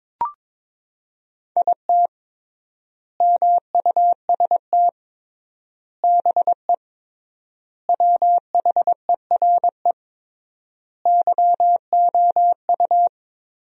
Morse code, heard as a keyed tone, about 22 wpm, 700 Hz. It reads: IT MUST BE WHERE YOU
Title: 3x Word-Spacing - Sentences from Top 200 Words - 22wpm